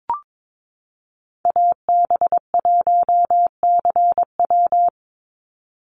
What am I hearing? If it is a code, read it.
Morse code, heard as a keyed tone, about 22 wpm, 700 Hz. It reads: AB1CW
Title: Call Signs Repeated 3x - 22wpm